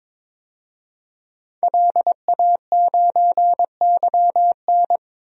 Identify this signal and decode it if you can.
Morse code, heard as a keyed tone, about 22 wpm, 700 Hz. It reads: LA9YN
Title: DXCC-100 Call Signs - 5 Characters - Repeated 2x - 22wpm